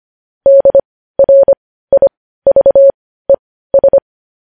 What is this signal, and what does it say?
Morse code, heard as a keyed tone, about 25 wpm, spaced as for 15 wpm, 550 Hz. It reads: DRIVES
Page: News Headlines in Morse Code at 15 WPM